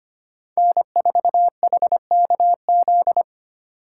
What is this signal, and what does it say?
Morse code, heard as a keyed tone, about 33 wpm, 700 Hz. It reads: N4HKZ